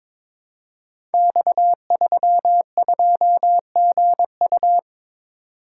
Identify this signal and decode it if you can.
Morse code, heard as a keyed tone, about 22 wpm, 700 Hz. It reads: X32GU
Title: Code Groups - 5 Letter:Numbers - 22wpm